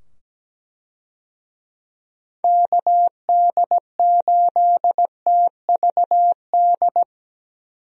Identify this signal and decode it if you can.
Morse code, heard as a keyed tone, about 17 wpm, 700 Hz. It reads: KD8TVD
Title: USA Call Signs - 6 Characters - 17wpm